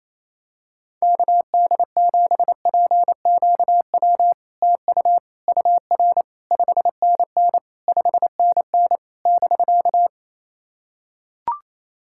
Morse code, heard as a keyed tone, about 28 wpm, 700 Hz. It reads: KD7PQW TU UR 5NN 5NN <BK>
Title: SOTA QSO Practice 28wpm